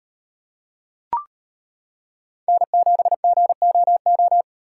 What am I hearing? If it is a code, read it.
Morse code, heard as a keyed tone, about 32 wpm, 700 Hz. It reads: N7GOO